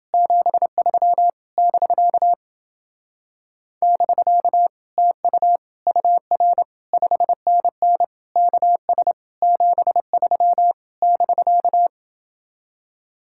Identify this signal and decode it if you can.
Morse code, heard as a keyed tone, about 30 wpm, 700 Hz. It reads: 73 <BK> <BK> TU UR 5NN KS 73 <BK>